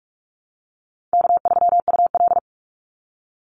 Morse code, heard as a keyed tone, about 44 wpm, 700 Hz. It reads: K3UL
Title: Easy Call Signs - 45wpm